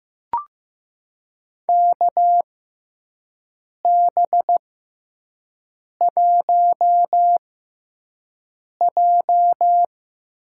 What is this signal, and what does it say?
Morse code, heard as a keyed tone, about 15 wpm, spaced as for 5 wpm, 700 Hz. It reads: KB1J
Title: USA Call Signs - 5 Characters - 6x Character Spacing - 15wpm